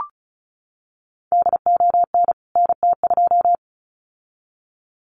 Morse code, heard as a keyed tone, about 34 wpm, 700 Hz. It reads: DON NT2
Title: CWT Exchange 35wpm